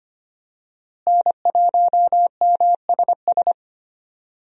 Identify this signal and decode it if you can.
Morse code, heard as a keyed tone, about 25 wpm, 700 Hz. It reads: N1MSS
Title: USA Call Signs - 5 Characters - Repeated 2x - 25wpm